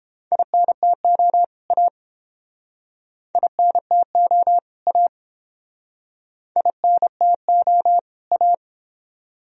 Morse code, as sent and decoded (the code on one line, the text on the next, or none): .. -. - --- / .- / .. -. - --- / .- / .. -. - --- / .-
INTO A INTO A INTO A